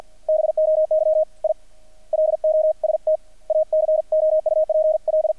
Morse code, heard as a keyed tone, about 50 wpm, 630 Hz. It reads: BOY I LOST A COUPL
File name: archived / CRSnet